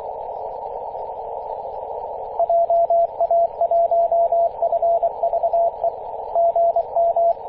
Morse code, heard as a keyed tone, about 24 wpm, 680 Hz, with radio noise. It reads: JA1FVE GM
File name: posts / 2016-may-04-2203z-e6ac-21-030mhz-cw